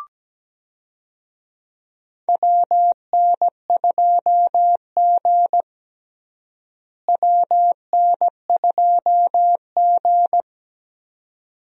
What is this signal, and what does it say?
Morse code, heard as a keyed tone, about 17 wpm, 700 Hz. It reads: WN2G WN2G
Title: USA Call Signs - 4 Characters - Repeated 2x - 17wpm